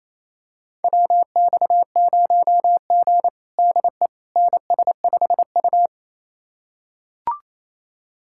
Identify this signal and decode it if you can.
Morse code, heard as a keyed tone, about 28 wpm, 690 Hz. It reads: WX0G DE NS5U
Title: USA Callsign DE Callsign - 4 Characters - 28wpm